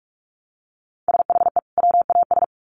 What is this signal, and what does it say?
Morse code, heard as a keyed tone, about 44 wpm, 700 Hz. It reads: SHE WAS
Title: Sets of 2 Words 45wpm